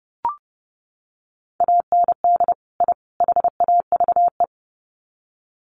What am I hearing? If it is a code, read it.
Morse code, heard as a keyed tone, about 30 wpm, 700 Hz. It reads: AND I HAVE
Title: Sets of 3 Words 30wpm